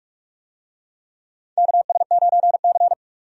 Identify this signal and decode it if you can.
Morse code, heard as a keyed tone, about 44 wpm, 700 Hz. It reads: KS9C
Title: USA Call Signs - 4 Characters - 45wpm